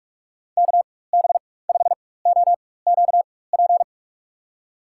Morse code, heard as a keyed tone, about 46 wpm, spaced as for 21 wpm, 700 Hz. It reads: KB5OQP
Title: USA Call Signs - 6 Characters - 4x Character Spacing - 45wpm